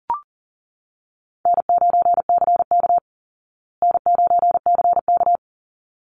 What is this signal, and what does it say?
Morse code, heard as a keyed tone, about 40 wpm, 700 Hz. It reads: N9CK N9CK
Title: Call Signs Repeated 3x - 40wpm